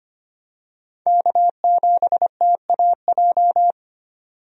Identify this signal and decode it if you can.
Morse code, heard as a keyed tone, about 25 wpm, 700 Hz. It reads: K7TAJ
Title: USA Call Signs - 5 Characters - 25wpm